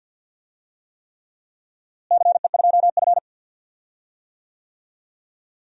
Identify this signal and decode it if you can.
Morse code, heard as a keyed tone, about 52 wpm, 700 Hz. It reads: CE2F
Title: DXCC-100 Call Signs - 4 Characters - Repeated 3x - 50wpm